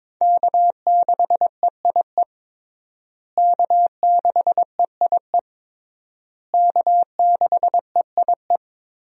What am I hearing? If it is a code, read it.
Morse code, heard as a keyed tone, about 22 wpm, 700 Hz. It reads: K6EIE K6EIE K6EIE
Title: USA Call Signs - 5 Characters - Repeated 3x - 22wpm